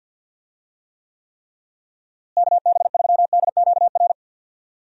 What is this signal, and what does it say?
Morse code, heard as a keyed tone, about 53 wpm, 700 Hz. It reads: KB3DQR
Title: USA Call Signs - 6 Characters - Repeated 3x - 50wpm